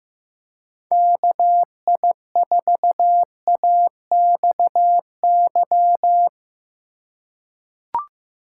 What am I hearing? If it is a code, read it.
Morse code, heard as a keyed tone, about 15 wpm, 700 Hz. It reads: KI4AXY
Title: USA Call Signs - 2x Character Spacing - 15wpm